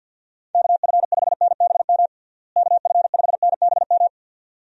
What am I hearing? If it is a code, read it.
Morse code, heard as a keyed tone, about 46 wpm, 700 Hz. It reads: KF5NBM KF5NBM
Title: USA Call Signs - 6 Characters - Repeated 2x - 50wpm